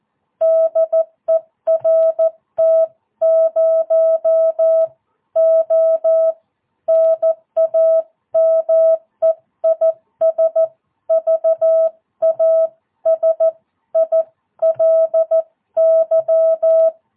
Morse code, heard as a keyed tone, about 14 wpm, spaced as for 12 wpm, 660 Hz. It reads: DE RT0O NAME IS VASILY
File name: jh8cbh / 2022cw